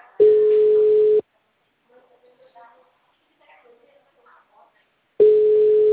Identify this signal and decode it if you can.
Ringback tone (Europe, 425 Hz), with speech.